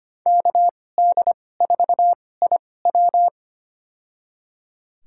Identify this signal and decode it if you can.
Morse code, heard as a keyed tone, about 25 wpm, spaced as for 18 wpm, 700 Hz. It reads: KD4IW
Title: USA Call Signs - 5 Characters - 2x Character Spacing - 25wpm